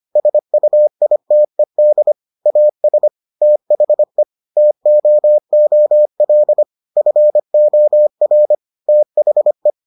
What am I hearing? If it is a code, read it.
Morse code, heard as a keyed tone, about 25 wpm, 600 Hz. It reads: SUITED AS THE TOOL FOR THE